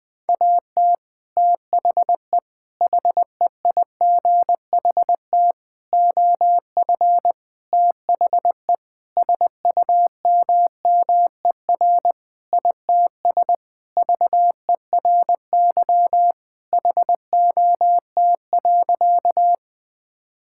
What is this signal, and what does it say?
Morse code, heard as a keyed tone, about 20 wpm, 700 Hz. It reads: AT THE HEIGHT OF THE SUMMER ITS VERY HOT.
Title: Sentences with 5-Word Phrases - 20wpm